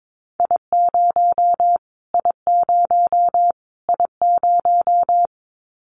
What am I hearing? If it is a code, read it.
Morse code, heard as a keyed tone, about 22 wpm, 700 Hz. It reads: I0 I0 I0
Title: Callsign Prefixes for Top 25 DXCC - Repeated 3x - 22wpm